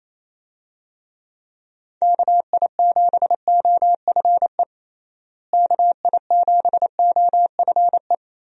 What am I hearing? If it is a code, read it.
Morse code, heard as a keyed tone, about 28 wpm, 700 Hz. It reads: KI7OFE KI7OFE
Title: USA Call Signs - 6 Characters - Repeated 3x - 28wpm